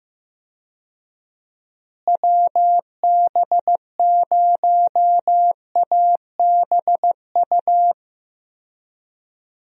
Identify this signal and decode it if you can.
Morse code, heard as a keyed tone, about 15 wpm, 700 Hz. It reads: WB0ABU
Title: USA Call Signs - 6 Characters - Repeated 3x - 15wpm